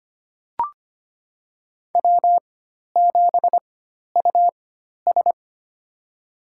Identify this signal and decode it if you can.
Morse code, heard as a keyed tone, about 25 wpm, spaced as for 12 wpm, 700 Hz. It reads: W7US
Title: USA Call Signs - 4 Characters - 4x Character Spacing - 25wpm